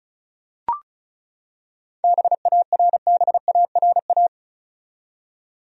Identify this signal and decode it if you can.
Morse code, heard as a keyed tone, about 34 wpm, 700 Hz. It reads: BARBARA
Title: USA Common Names - 35wpm